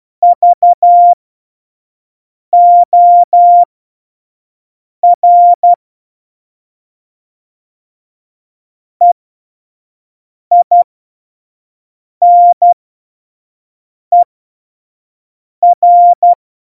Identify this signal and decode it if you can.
Morse code, heard as a keyed tone, about 12 wpm, spaced as for 5 wpm, 700 Hz. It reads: VOR EINER
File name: Morsekurs / Witze / Witz 03 April 2018, 25 F70